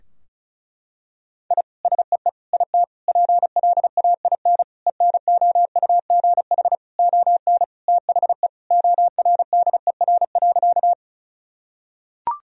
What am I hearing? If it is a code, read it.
Morse code, heard as a keyed tone, about 34 wpm, 700 Hz. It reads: I SEE IT PLAIN ENOUGH ON THE ORDER.
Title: Sentences from Top 1000 Words - Review Part 2 - 35wpm